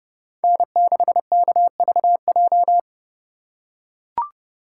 Morse code, heard as a keyed tone, about 30 wpm, 700 Hz. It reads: N6KVJ
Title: USA Call Signs - 5 Characters - 2x Character Spacing - 30wpm